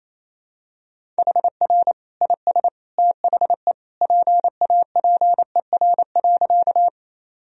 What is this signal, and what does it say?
Morse code, heard as a keyed tone, about 28 wpm, 700 Hz. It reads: HR IS THE PAPER.
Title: Sentences from Top 300 Words and Abbreviations 28wpm